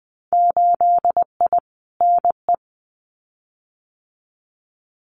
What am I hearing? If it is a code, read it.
Morse code, heard as a keyed tone, about 20 wpm, 700 Hz. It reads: 8I NE
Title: Winter Field Day Exchanges Encore 20wpm